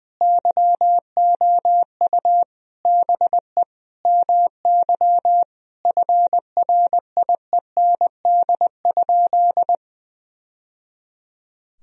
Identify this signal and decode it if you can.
Morse code, heard as a keyed tone, about 21 wpm, 700 Hz. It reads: YOU BE MY FRIEND?
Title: Sentences from Top 1000 Words - Review Part 3 - 20wpm